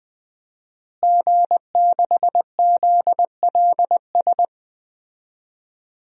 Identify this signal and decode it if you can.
Morse code, heard as a keyed tone, about 20 wpm, 700 Hz. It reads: G6ZLS
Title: DXCC-100 Call Signs - 5 Characters - Repeated 3x - 20wpm